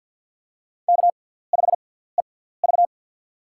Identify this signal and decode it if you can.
Morse code, heard as a keyed tone, about 52 wpm, spaced as for 17 wpm, 700 Hz. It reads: K5EV
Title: USA Call Signs - 4 Characters - 6x Character Spacing - 50wpm